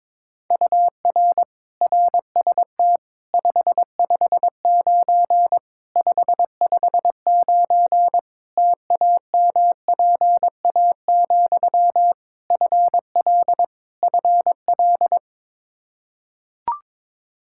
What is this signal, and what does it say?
Morse code, heard as a keyed tone, about 22 wpm, 700 Hz. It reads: UR RST 559 559 TAMPA, FL FL
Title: Practice QSO - Exchange 1 - Beginning - 22wpm